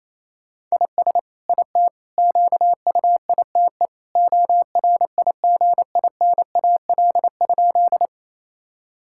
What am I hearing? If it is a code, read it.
Morse code, heard as a keyed tone, about 28 wpm, 700 Hz. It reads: IS IT QUITE ORIGINAL?
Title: Sentences from Top 1000 Words - Review Part 6 - 28wpm